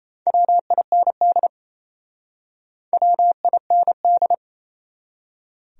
Morse code, heard as a keyed tone, about 37 wpm, 700 Hz. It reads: WIND WIND